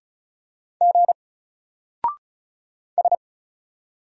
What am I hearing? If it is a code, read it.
Morse code, heard as a keyed tone, about 35 wpm, 700 Hz. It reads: G S
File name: Single-Letter-Number-35wpm